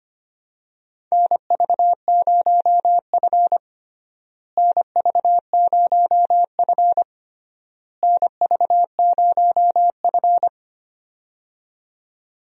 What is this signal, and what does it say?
Morse code, heard as a keyed tone, about 25 wpm, 700 Hz. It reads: NV0F NV0F NV0F
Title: USA Call Signs - 4 Characters - Repeated 3x - 25wpm